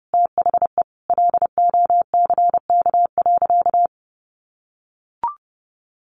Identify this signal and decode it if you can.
Morse code, heard as a keyed tone, about 30 wpm, 700 Hz. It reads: THE LOCK.
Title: Sentences from Top 1300 Words 30wpm